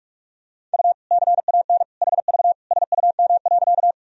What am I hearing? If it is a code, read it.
Morse code, heard as a keyed tone, about 46 wpm, 700 Hz. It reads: U CAN HV SUM.
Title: Sentences from Top 100 Words and Abbreviations 45wpm